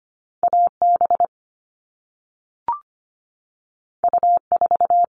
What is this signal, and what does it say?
Morse code, heard as a keyed tone, about 25 wpm, 700 Hz. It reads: AB U4
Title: All Cut Numbers - 2-Digit 25wpm